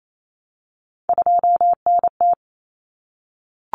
Morse code, heard as a keyed tone, about 28 wpm, 700 Hz. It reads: 2NT
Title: Common Cut Numbers - 3-Digit 28wpm